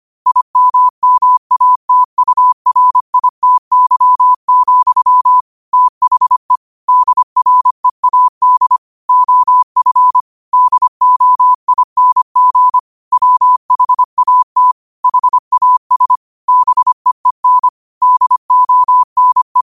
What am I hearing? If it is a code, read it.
Morse code, heard as a keyed tone, about 25 wpm, 1000 Hz. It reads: IMMATURITY, THE DREAD OF DOING WHAT HAS BEEN DONE